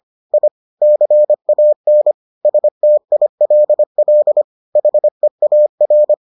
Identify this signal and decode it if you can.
Morse code, heard as a keyed tone, about 25 wpm, 600 Hz. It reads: I CAN STILL HEAR